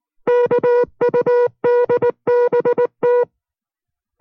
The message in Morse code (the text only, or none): KUDBT